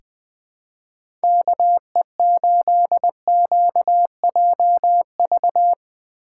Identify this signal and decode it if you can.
Morse code, heard as a keyed tone, about 20 wpm, 700 Hz. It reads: KE8QJV
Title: 2023 POTA Activator Call Signs - 20wpm